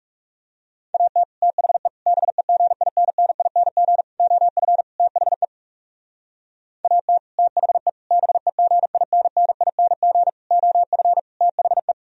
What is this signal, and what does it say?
Morse code, heard as a keyed tone, about 39 wpm, 700 Hz. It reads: AT THE BEGINNING OF THE AT THE BEGINNING OF THE